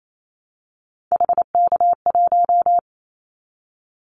Morse code, heard as a keyed tone, about 28 wpm, 700 Hz. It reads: HK1
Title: Callsign Prefixes for Top 100 DXCC - 28wpm